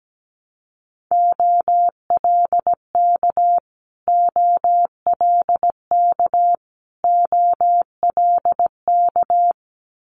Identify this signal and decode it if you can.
Morse code, heard as a keyed tone, about 17 wpm, 690 Hz. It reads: OLK OLK OLK
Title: Multi-letter Phonemes Repeated 3x 17wpm